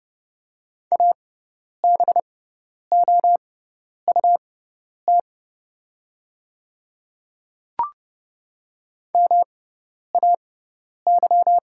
Morse code, heard as a keyed tone, about 30 wpm, spaced as for 10 wpm, 700 Hz. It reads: ABOUT MAY
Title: 6x Character Spacing - Top 100 Words - 30wpm